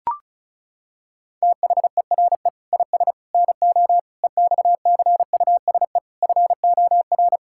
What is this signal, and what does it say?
Morse code, heard as a keyed tone, about 36 wpm, 700 Hz. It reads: THERE IS NO EXCUSE FOR
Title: Sentences from Top 1000 Words - Review Part 6 - 35wpm